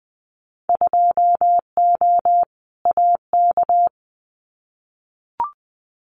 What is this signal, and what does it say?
Morse code, heard as a keyed tone, about 20 wpm, 700 Hz. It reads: 2O AK
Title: Winter Field Day Exchanges Encore 20wpm